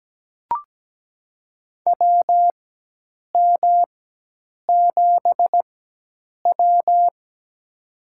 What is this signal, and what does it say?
Morse code, heard as a keyed tone, about 17 wpm, spaced as for 8 wpm, 700 Hz. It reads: WM7W
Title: USA Call Signs - 4 Characters - 4x Character Spacing - 17wpm